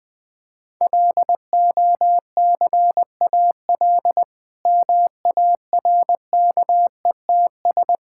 Morse code, heard as a keyed tone, about 20 wpm, 700 Hz. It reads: LOCAL MARKETS
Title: Difficult Sentences with Hints - 20wpm